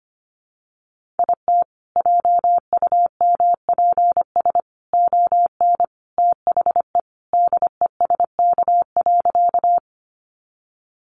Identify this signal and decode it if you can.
Morse code, heard as a keyed tone, about 25 wpm, 700 Hz. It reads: IT JUMPS ON THE DESK.
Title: Additional Easy Phrases 25wpm